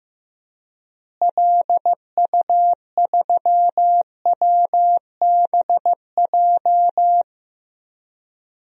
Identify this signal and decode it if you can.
Morse code, heard as a keyed tone, about 15 wpm, 700 Hz. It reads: LU3WBJ
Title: DXCC-100 Call Signs - 6 Characters - Repeated 3x - 15wpm